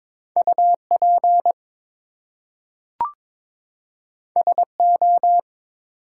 Morse code, heard as a keyed tone, about 22 wpm, 700 Hz. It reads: UP SO
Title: Two Letter Words - 22wpm